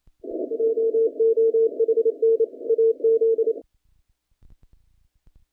Morse code, heard as a keyed tone, about 30 wpm, 450 Hz. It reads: JOHN AZ
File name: contests / naqp-cw-k5tr-op-k5pi- -0300